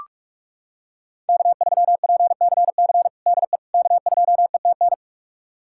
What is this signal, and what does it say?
Morse code, heard as a keyed tone, about 46 wpm, 700 Hz. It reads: K3PCC DE K2ETN
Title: USA Callsign DE Callsign - 5 Characters - 45wpm